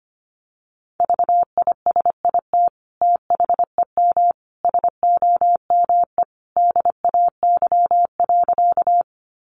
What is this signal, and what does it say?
Morse code, heard as a keyed tone, about 25 wpm, 700 Hz. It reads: VISIT THEM SOME DAY.
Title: Sentences from Top 900 Words 25wpm